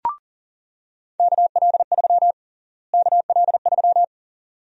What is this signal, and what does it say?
Morse code, heard as a keyed tone, about 40 wpm, 700 Hz. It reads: KL3 KL3
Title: USA Call Sign Prefixes - Repeated 2x - 40wpm